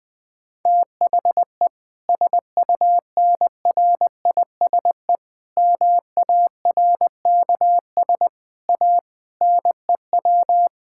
Morse code, heard as a keyed tone, about 20 wpm, 700 Hz. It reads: THE SUNRISE MARKS A NEW
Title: Difficult Sentences with Hints - 20wpm